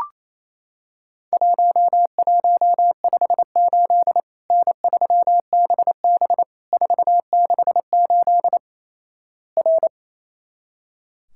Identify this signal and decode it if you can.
Morse code, heard as a keyed tone, about 28 wpm, 700 Hz. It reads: 1158 N3BB 468 R
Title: Worked All Europe DX Contest - QTC Exchange - 28wpm